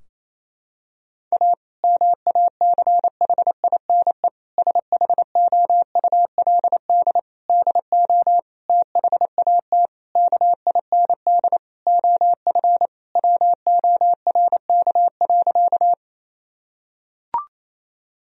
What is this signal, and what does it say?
Morse code, heard as a keyed tone, about 27 wpm, 700 Hz. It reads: A MACHINE SHOULD DO THAT KIND OF WORK.
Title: Sentences from Top 1000 Words - Review Part 2 - 28wpm